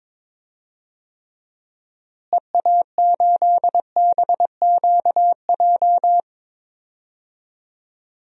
Morse code, heard as a keyed tone, about 22 wpm, 700 Hz. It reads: EA8BQJ
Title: DXCC-100 Call Signs - 6 Characters - Repeated 2x - 22wpm